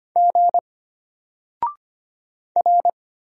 Morse code, heard as a keyed tone, about 25 wpm, 700 Hz. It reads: G R
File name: Single-Letter-Number-25wpm